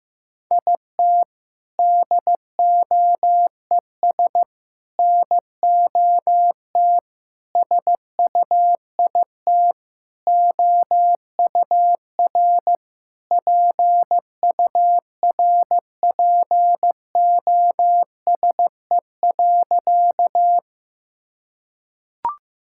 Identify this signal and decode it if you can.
Morse code, heard as a keyed tone, about 15 wpm, 700 Hz. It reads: IT DOES NOT SUIT OUR PURPOSE.
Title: Sentences from Top 1600 Words 15wpm